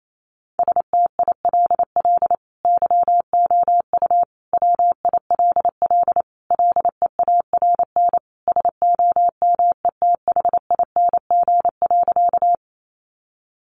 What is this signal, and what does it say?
Morse code, heard as a keyed tone, about 28 wpm, 700 Hz. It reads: STILL YOU WILL LEARN SOMETHING.
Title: Sentences from Top 600 Words 28wpm